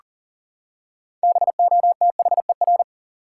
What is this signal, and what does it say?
Morse code, heard as a keyed tone, about 40 wpm, 700 Hz. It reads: BOTHER
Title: Six Letter Words - 40wpm